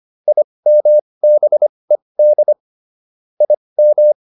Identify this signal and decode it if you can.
Morse code, heard as a keyed tone, about 25 wpm, spaced as for 20 wpm, 600 Hz. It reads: IMBED IM